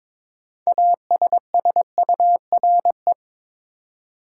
Morse code, heard as a keyed tone, about 22 wpm, 700 Hz. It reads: ASSURE
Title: Six Letter Words - 22wpm